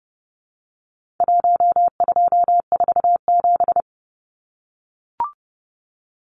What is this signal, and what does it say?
Morse code, heard as a keyed tone, about 30 wpm, 700 Hz. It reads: 1247
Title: Numbers - Contest Style - 4-Digit 30wpm